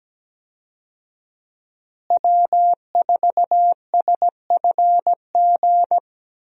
Morse code, heard as a keyed tone, about 17 wpm, 700 Hz. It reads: W4SFG
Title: USA Call Signs - 5 Characters - Repeated 2x - 17wpm